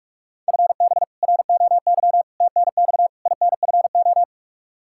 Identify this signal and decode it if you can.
Morse code, heard as a keyed tone, about 48 wpm, 700 Hz. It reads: FB ROY TNX INFO
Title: K1USN SST QSO Practice 45wpm